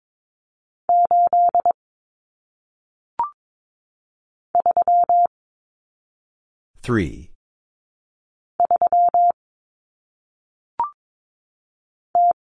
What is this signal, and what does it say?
Morse code, heard as a keyed tone, about 22 wpm, 700 Hz. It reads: 8 3 3 T
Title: Common Cut Numbers - 1-Digit 22wpm